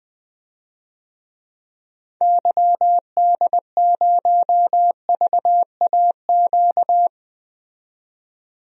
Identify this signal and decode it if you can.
Morse code, heard as a keyed tone, about 21 wpm, 700 Hz. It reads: YD0VAQ
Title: DXCC-100 Call Signs - 6 Characters - 20wpm